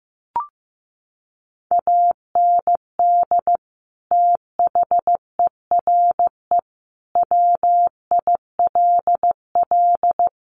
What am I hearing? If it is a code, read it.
Morse code, heard as a keyed tone, about 15 wpm, 700 Hz. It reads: AND THERE WILL
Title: Sentences from Top 1500 Words 15wpm